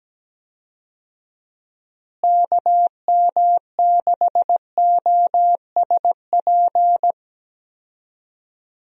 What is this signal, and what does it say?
Morse code, heard as a keyed tone, about 17 wpm, 700 Hz. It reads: KM6OSP
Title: USA Call Signs - 6 Characters - Repeated 2x - 17wpm